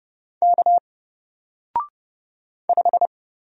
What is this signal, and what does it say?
Morse code, heard as a keyed tone, about 30 wpm, 700 Hz. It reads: K 5
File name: Single-Letter-Number-30wpm